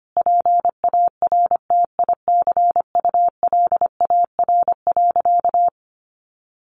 Morse code, heard as a keyed tone, about 25 wpm, 700 Hz. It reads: PARTICULAR.
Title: Sentences from Top 900 Words 25wpm